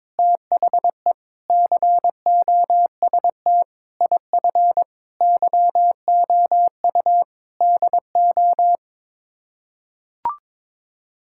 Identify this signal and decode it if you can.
Morse code, heard as a keyed tone, about 22 wpm, 700 Hz. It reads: THE COST IF YOU DO
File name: practice / 22wpm / Sets of 8 Words - 22wpm